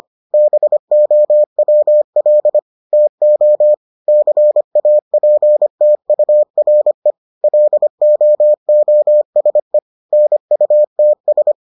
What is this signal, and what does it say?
Morse code, heard as a keyed tone, about 25 wpm, 600 Hz. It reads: BOWL TO CAPTURE LOOSE NUTS